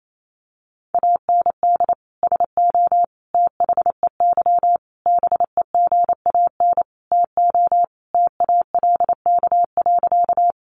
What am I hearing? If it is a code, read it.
Morse code, heard as a keyed tone, about 28 wpm, 700 Hz. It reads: AND SO THEY BEGAN TO TALK.
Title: Sentences from Top 400 Words 28wpm